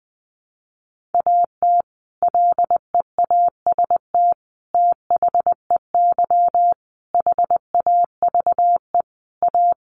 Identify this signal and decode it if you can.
Morse code, heard as a keyed tone, about 20 wpm, 700 Hz. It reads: AT LEAST THEY HAVE A
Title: Sentences from Top 1500 Words 20wpm